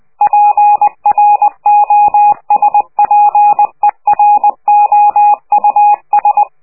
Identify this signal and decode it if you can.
Morse code, heard as a keyed tone, about 20 wpm, 830 Hz. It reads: PROSPEROUS
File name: worldcoastal / scheveningenradio / morsepch